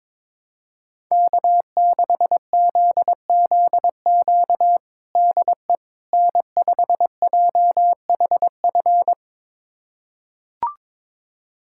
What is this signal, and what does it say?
Morse code, heard as a keyed tone, about 22 wpm, 700 Hz. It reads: K6ZZQ DE N5JHF
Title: USA Callsign DE Callsign - 5 Characters - 22wpm